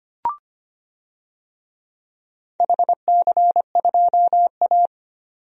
Morse code, heard as a keyed tone, about 25 wpm, 700 Hz. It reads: HC2A
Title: DXCC-100 Call Signs - 4 Characters - 25wpm